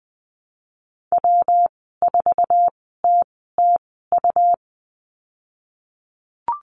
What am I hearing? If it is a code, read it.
Morse code, heard as a keyed tone, about 19 wpm, spaced as for 14 wpm, 700 Hz. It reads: W4TTU
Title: 147 Lesson 24g - Letter y - Call Signs 20wpm